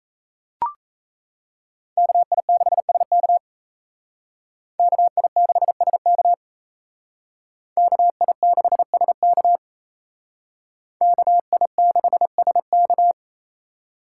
Morse code, heard as a keyed tone, about 36 wpm, 700 Hz. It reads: KI6SK KI6SK KI6SK KI6SK